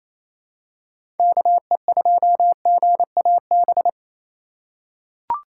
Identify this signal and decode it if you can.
Morse code, heard as a keyed tone, about 27 wpm, 700 Hz. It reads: KE2GAB
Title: USA Call Signs - 6 Characters - 3x Character Spacing - 28wpm